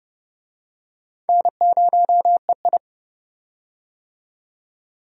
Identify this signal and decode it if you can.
Morse code, heard as a keyed tone, about 30 wpm, 700 Hz. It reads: N0EI
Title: USA Call Signs - 4 Characters - 30wpm